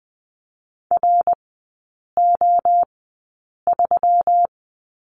Morse code, heard as a keyed tone, about 20 wpm, spaced as for 8 wpm, 700 Hz. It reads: RO3
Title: 178 Lesson 29e - Slash - Three Characters 20wpm